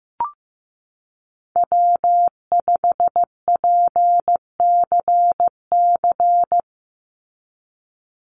Morse code, heard as a keyed tone, about 15 wpm, 700 Hz. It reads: W5PCC
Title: USA Call Signs 15wpm